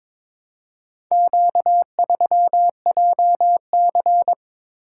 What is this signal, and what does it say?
Morse code, heard as a keyed tone, about 22 wpm, 700 Hz. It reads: Q3JC
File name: practice / 22wpm / Four Letter-Number Combinations - 22wpm